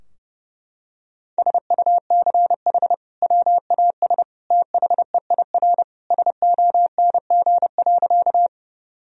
Morse code, heard as a keyed tone, about 30 wpm, 700 Hz. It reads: SUCH WAS THEIR SONG.
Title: Sentences from Top 1000 Words - Review Part 2 - 30wpm